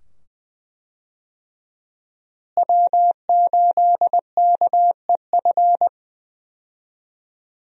Morse code, heard as a keyed tone, about 20 wpm, 700 Hz. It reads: W8KEF
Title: USA Call Signs - 5 Characters - Repeated 2x - 20wpm